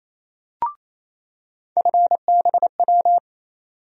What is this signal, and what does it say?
Morse code, heard as a keyed tone, about 28 wpm, 700 Hz. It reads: FBW
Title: USA Call Sign Postfixes - 3 Letters 28wpm